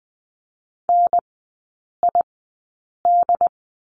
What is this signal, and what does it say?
Morse code, heard as a keyed tone, about 20 wpm, spaced as for 8 wpm, 700 Hz. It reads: NID
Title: 067 Lesson 12e - Letter d - Three Characters 20wpm